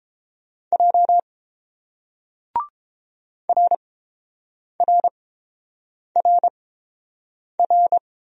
Morse code, heard as a keyed tone, about 29 wpm, spaced as for 7 wpm, 700 Hz. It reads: J RRRR